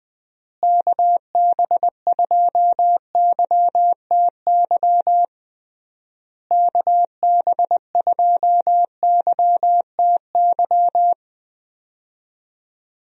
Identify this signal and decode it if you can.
Morse code, heard as a keyed tone, about 20 wpm, 700 Hz. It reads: KB2YTY KB2YTY
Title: USA Call Signs - 6 Characters - Repeated 3x - 20wpm